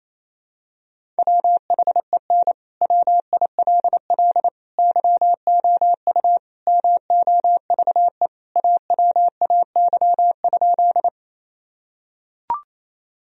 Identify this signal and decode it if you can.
Morse code, heard as a keyed tone, about 28 wpm, 700 Hz. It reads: WHEN WILL YOU MOVE AWAY?
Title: Sentences from Top 1000 Words - Review Part 1 - 28wpm